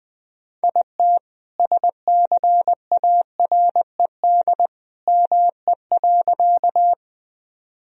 Morse code, heard as a keyed tone, about 20 wpm, 700 Hz. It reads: IT SCARED ME.
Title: Sentences from Top 1000 Words - Review Part 4 - 20wpm